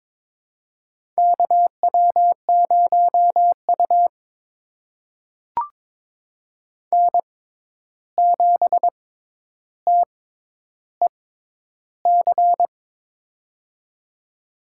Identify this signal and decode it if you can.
Morse code, heard as a keyed tone, about 21 wpm, spaced as for 10 wpm, 700 Hz. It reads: KW0U N7TEC
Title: USA Call Signs - 6x Character Spacing - 22wpm